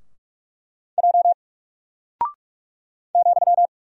Morse code, heard as a keyed tone, about 44 wpm, 700 Hz. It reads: J ,
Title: Single Character (Letter:Number:Punctuation:Prosign) 45wpm